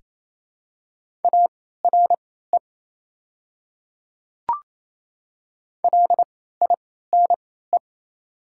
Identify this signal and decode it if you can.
Morse code, heard as a keyed tone, about 27 wpm, spaced as for 16 wpm, 700 Hz. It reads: ARE LINE
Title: 3x Character Spacing - Top 300 Words - 28wpm